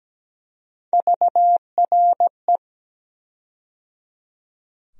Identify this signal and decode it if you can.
Morse code, heard as a keyed tone, about 17 wpm, 700 Hz. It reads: VRE
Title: USA Call Sign Postfixes - 3 Letters 17wpm